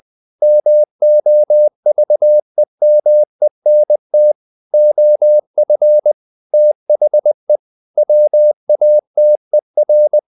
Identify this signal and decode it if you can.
Morse code, heard as a keyed tone, about 20 wpm, 600 Hz. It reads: MOVEMENT OF THE WATER
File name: the old man and the sea / the old man and the sea part 1 20wpm